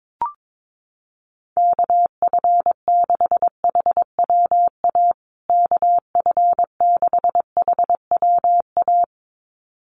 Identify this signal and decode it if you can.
Morse code, heard as a keyed tone, about 22 wpm, 700 Hz. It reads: KF6HWA KF6HWA
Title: USA Call Signs Repeated 2x 22wpm